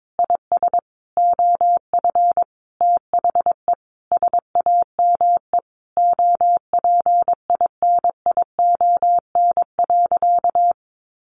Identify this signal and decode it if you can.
Morse code, heard as a keyed tone, about 22 wpm, 700 Hz. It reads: IS OF THE SAME OPINION.
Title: Sentences from Top 1200 Words 22wpm